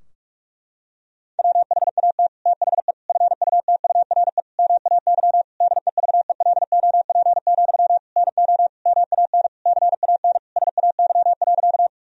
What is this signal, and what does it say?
Morse code, heard as a keyed tone, about 46 wpm, 700 Hz. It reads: WHAT THE FUTURE MAY DEVELOP, NO MAN CAN SAY.
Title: Sentences from Top 1000 Words - Review Part 2 - 45wpm